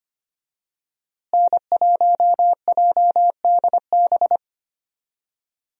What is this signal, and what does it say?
Morse code, heard as a keyed tone, about 25 wpm, 700 Hz. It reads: N1JDB
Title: USA Call Signs - 5 Characters - Repeated 2x - 25wpm